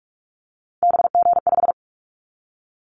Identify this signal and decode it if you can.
Morse code, heard as a keyed tone, about 44 wpm, 700 Hz. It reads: BG5
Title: Callsign Prefixes for Top 100 DXCC - 45wpm